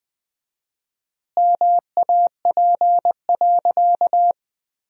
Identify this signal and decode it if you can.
Morse code, heard as a keyed tone, about 21 wpm, 700 Hz. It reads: MAP.
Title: 4x Word-Spacing - Sentences from Top 400 Words - 20wpm